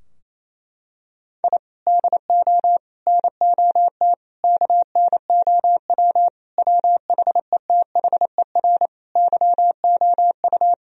Morse code, heard as a keyed tone, about 28 wpm, 700 Hz. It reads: I DO NOT KNOW WHETHER YOU
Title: Sentences from Top 1000 Words - Review Part 3 - 28wpm